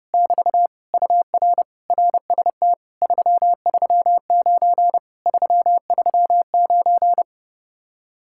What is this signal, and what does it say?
Morse code, heard as a keyed tone, about 29 wpm, 700 Hz. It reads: <BT> UR RST 339 339